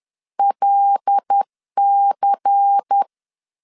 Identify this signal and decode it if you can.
Morse code, heard as a keyed tone, about 10 wpm, 800 Hz. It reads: LC